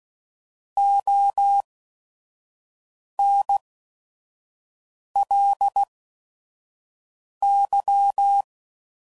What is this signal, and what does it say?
Morse code, heard as a keyed tone, about 16 wpm, spaced as for 5 wpm, 800 Hz. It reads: ONLY
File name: CW Code Groups @5WPM / Weather Sets / Weather 4 28